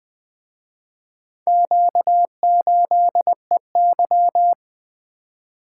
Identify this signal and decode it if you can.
Morse code, heard as a keyed tone, about 24 wpm, 700 Hz. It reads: Q8EY
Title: Speed-Racing - Four Letter-Number Combinations - 20wpm